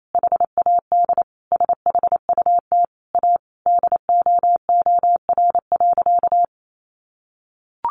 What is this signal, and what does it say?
Morse code, heard as a keyed tone, about 28 wpm, 700 Hz. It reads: HAD SHUT A DOOR.
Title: A Sound Of Thunder 28wpm